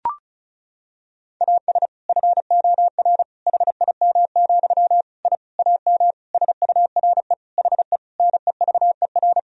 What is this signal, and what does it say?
Morse code, heard as a keyed tone, about 36 wpm, 700 Hz. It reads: AS FOR HIM, I AM SURE HE NEVER
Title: Sentences from Top 1000 Words - Review Part 2 - 35wpm